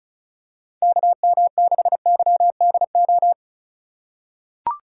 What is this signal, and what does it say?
Morse code, heard as a keyed tone, about 34 wpm, 700 Hz. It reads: KM6YDO
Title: USA Call Signs - 4x Character Spacing - 35wpm